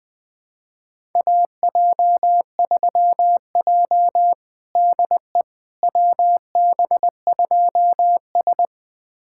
Morse code, heard as a keyed tone, about 21 wpm, 700 Hz. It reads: AJ3J DE WB2S
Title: USA Callsign DE Callsign - 4 Characters - 20wpm